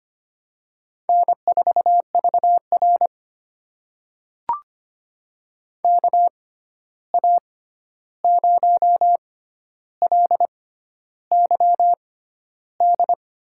Morse code, heard as a keyed tone, about 25 wpm, spaced as for 11 wpm, 700 Hz. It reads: N4VR KA0LYD
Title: USA Call Signs - 6x Character Spacing - 25wpm